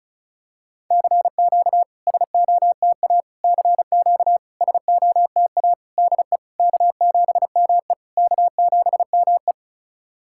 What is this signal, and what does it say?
Morse code, heard as a keyed tone, about 36 wpm, 690 Hz. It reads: CQ SOTA CQ SOTA DE K7ME K7ME
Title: SOTA QSO Practice 35wpm